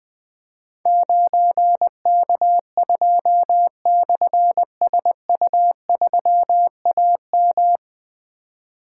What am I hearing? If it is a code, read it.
Morse code, heard as a keyed tone, about 20 wpm, 700 Hz. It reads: 9K2/SU3AM
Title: Difficult Call Signs 20wpm